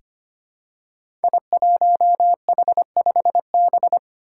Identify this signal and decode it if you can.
Morse code, heard as a keyed tone, about 25 wpm, 700 Hz. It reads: I1H5B
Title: Five Challenging Letter-Number Combinations 25wpm